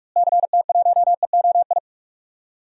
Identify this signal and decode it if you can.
Morse code, heard as a keyed tone, about 41 wpm, 700 Hz. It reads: CT1EOI
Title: DXCC-100 Call Signs - 6 Characters - Repeated 3x - 45wpm